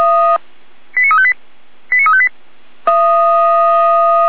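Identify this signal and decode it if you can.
Five-tone selective call (ZVEI), heard as quick runs of five tones, tones of about 70 ms. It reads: 89358 89358
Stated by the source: ZVEI five-tone selective call 89230 received with an SDR by the feuerwehr-orlamuende fire brigade.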